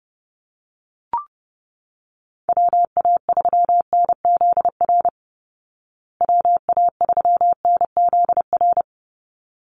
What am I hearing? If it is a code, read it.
Morse code, heard as a keyed tone, about 30 wpm, 700 Hz. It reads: WA3NZR WA3NZR
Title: Call Signs Repeated 3x - 30wpm